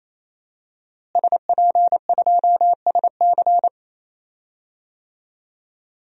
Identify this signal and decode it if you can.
Morse code, heard as a keyed tone, about 28 wpm, 700 Hz. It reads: SP2SC
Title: DXCC-100 Call Signs - 5 Characters - Repeated 3x - 28wpm